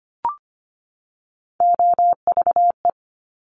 Morse code, heard as a keyed tone, about 25 wpm, 700 Hz. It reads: OVE
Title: Top Three Letter Combinations in Words 25wpm